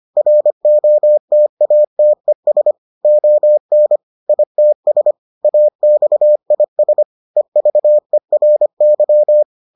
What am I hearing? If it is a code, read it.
Morse code, heard as a keyed tone, about 25 wpm, 600 Hz. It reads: ROTATES ON ITS AXIS EVERY